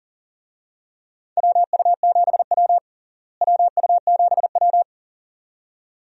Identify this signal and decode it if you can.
Morse code, heard as a keyed tone, about 40 wpm, 700 Hz. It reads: WU7W WU7W
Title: USA Call Signs - 4 Characters - Repeated 2x - 40wpm